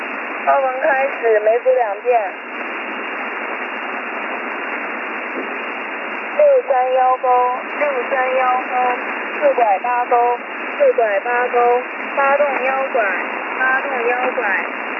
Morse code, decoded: none